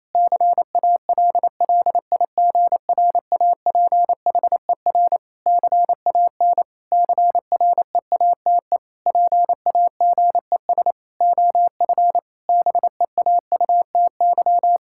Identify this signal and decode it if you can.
Morse code, heard as a keyed tone, about 28 wpm, 700 Hz. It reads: CALLIGRAPHER CAN CREATE PAGES OF BEAUTY
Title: Pangram Challenge v2 - 28wpm